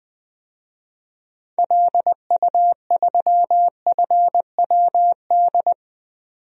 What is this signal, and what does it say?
Morse code, heard as a keyed tone, about 20 wpm, 700 Hz. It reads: LU3FWD
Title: DXCC-100 Call Signs - 6 Characters - 20wpm